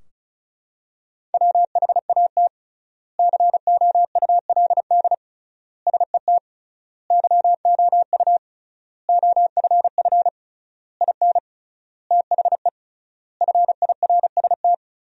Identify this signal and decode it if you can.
Morse code, heard as a keyed tone, about 34 wpm, 700 Hz. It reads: WHAT COULD SET YOU OFF IN THE FIRST
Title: 3x Word-Spacing - Sentences from Top 200 Words - 35wpm